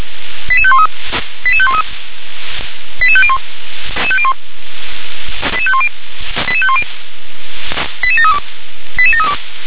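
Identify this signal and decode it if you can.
Five-tone selective call (ZVEI), heard as quick runs of five tones, tones of about 70 ms. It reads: 88513 88513 88501 88510 88510 88732 88732
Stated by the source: ZVEI five-tone selective call 88500 received with an SDR.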